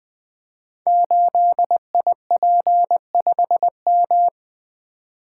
Morse code, heard as a keyed tone, about 20 wpm, 700 Hz. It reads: 8IP5M
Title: Code Groups - 5 Letter:Numbers - 20wpm